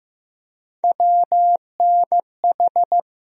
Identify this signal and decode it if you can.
Morse code, heard as a keyed tone, about 15 wpm, 700 Hz. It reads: WNH
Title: USA Call Sign Postfixes - 3 Letters 15wpm